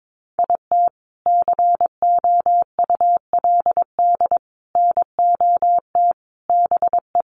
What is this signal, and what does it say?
Morse code, heard as a keyed tone, about 22 wpm, 700 Hz. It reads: IT COULD NOT BE
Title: Sentences from Top 1200 Words 22wpm